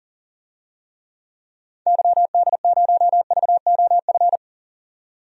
Morse code, heard as a keyed tone, about 37 wpm, 700 Hz. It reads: YD0VOF
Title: DXCC-100 Call Signs - 6 Characters - 40wpm